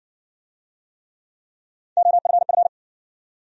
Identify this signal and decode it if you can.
Morse code, heard as a keyed tone, about 60 wpm, 700 Hz. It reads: OFF